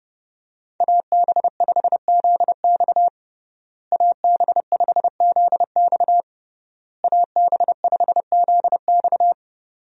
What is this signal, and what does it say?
Morse code, heard as a keyed tone, about 32 wpm, 700 Hz. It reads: AB5ZX AB5ZX AB5ZX
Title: USA Call Signs - 5 Characters - Repeated 3x - 30wpm